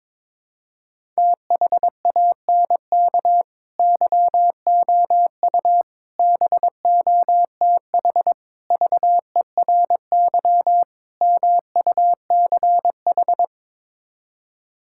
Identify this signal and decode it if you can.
Morse code, heard as a keyed tone, about 22 wpm, 700 Hz. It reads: THANK YOU BOTH VERY MUCH
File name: Build A Phrase - 5-Word Phrases - 22wpm